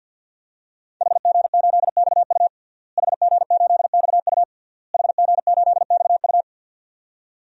Morse code, heard as a keyed tone, about 51 wpm, 700 Hz. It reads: HG8XU HG8XU HG8XU
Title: DXCC-100 Call Signs - 5 Characters - Repeated 3x - 50wpm